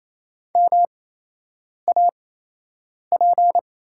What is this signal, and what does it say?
Morse code, heard as a keyed tone, about 28 wpm, spaced as for 8 wpm, 700 Hz. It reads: MAP
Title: 8x Character Spacing - Top 500 Words - 28wpm